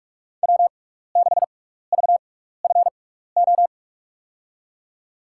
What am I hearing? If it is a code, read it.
Morse code, heard as a keyed tone, about 47 wpm, spaced as for 16 wpm, 700 Hz. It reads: W6VFO
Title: USA Call Signs - 6x Character Spacing - 45wpm